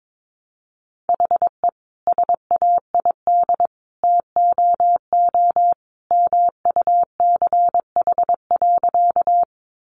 Morse code, heard as a keyed tone, about 22 wpm, 700 Hz. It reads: HE SAID TOO MUCH.
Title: Sentences from Top 200 Words 22wpm